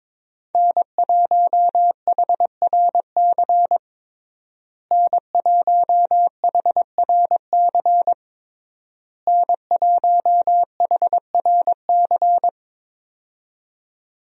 Morse code, heard as a keyed tone, about 21 wpm, 700 Hz. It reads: N1HRC N1HRC N1HRC
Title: USA Call Signs - 5 Characters - Repeated 3x - 22wpm